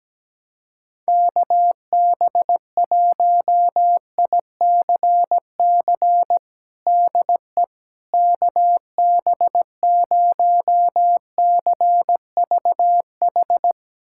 Morse code, heard as a keyed tone, about 17 wpm, 700 Hz. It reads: KB1ICC DE KB0CVH
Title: USA Callsign DE Callsign 17wpm